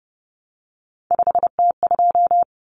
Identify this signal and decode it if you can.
Morse code, heard as a keyed tone, about 30 wpm, 700 Hz. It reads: 5T2
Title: Common Cut Numbers - 3-Digit 30wpm